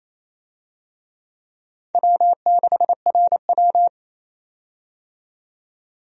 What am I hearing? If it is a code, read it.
Morse code, heard as a keyed tone, about 28 wpm, 700 Hz. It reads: W6RW
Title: USA Call Signs - 4 Characters - 28wpm